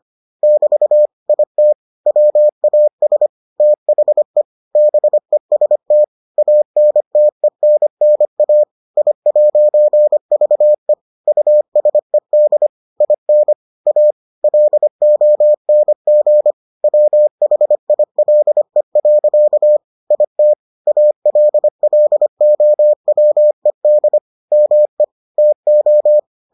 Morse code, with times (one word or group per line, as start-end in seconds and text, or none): 0.4-1.1 <BT>
1.3-1.7 IT
2.1-3.3 WAS
3.6-4.4 THE
4.8-6.0 BEST
6.4-8.6 ANTENNA
9.0-10.9 I'VE
11.3-12.7 USED
13.0-13.5 IN
13.9-14.1 A
14.4-16.5 LONG
16.8-19.8 WHILE.
20.1-20.5 IT
20.9-24.2 ALLOWED
24.5-25.0 ME
25.4-26.2 TO